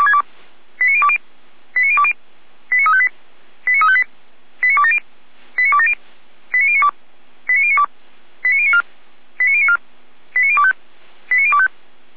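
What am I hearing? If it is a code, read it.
Five-tone selective call (ZVEI), heard as quick runs of five tones, tones of about 70 ms. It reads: 89020 89020 89358 89358 89270 89270 89092 89092 89094 89094 89025 89025
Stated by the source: ZVEI five-tone selective call 89280 received on SDR.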